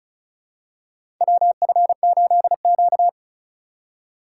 Morse code, heard as a keyed tone, about 36 wpm, 700 Hz. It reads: WF8Q
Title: USA Call Signs - 4 Characters - 2x Character Spacing - 35wpm